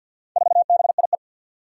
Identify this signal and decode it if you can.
Morse code, heard as a keyed tone, about 47 wpm, 700 Hz. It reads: 4BIE
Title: Four Challenging Letter-Number Combinations 50wpm